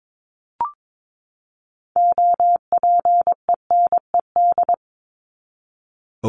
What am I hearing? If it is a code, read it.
Morse code, heard as a keyed tone, about 22 wpm, 700 Hz. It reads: OPENED
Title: ED Suffix - Top 100 Words with Base Words 22wpm